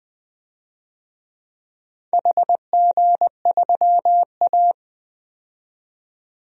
Morse code, heard as a keyed tone, about 20 wpm, 700 Hz. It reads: HG3A
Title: DXCC-100 Call Signs - 4 Characters - 20wpm